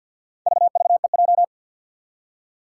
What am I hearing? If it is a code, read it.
Morse code, heard as a keyed tone, about 53 wpm, 700 Hz. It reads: VVEJ
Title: Four Challenging Letter-Number Combinations 50wpm